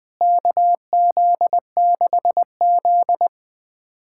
Morse code, heard as a keyed tone, about 21 wpm, 700 Hz. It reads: KZ6Z
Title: USA Callsign DE Callsign - 4 Characters - 20wpm